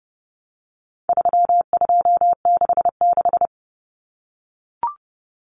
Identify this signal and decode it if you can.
Morse code, heard as a keyed tone, about 30 wpm, 700 Hz. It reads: 3266
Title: Numbers - 4-Digit 30wpm